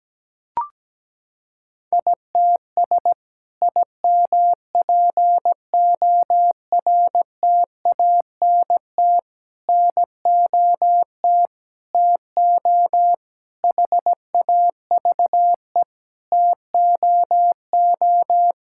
Morse code, heard as a keyed tone, about 17 wpm, 700 Hz. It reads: ITS IMPORTANT NOT TO HAVE TOO
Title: Sentences from Top 1000 Words - Review Part 5 - 17wpm